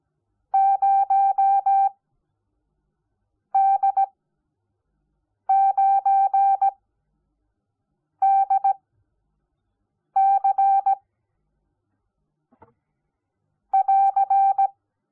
Morse code, heard as a keyed tone, about 17 wpm, spaced as for 5 wpm, 780 Hz. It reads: 0D9DC <AR>
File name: Onlinelehrgang / morsen / DJ4UF Morsen05